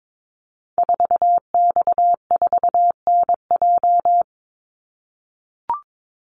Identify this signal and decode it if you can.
Morse code, heard as a keyed tone, about 22 wpm, 700 Hz. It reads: 4X4NJ
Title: Call Signs - 22wpm